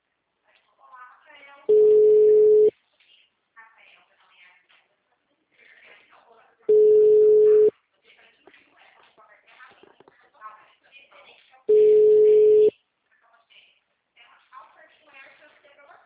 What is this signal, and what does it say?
Ringback tone (Europe, 425 Hz), with speech.
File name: out- -2020- - - .28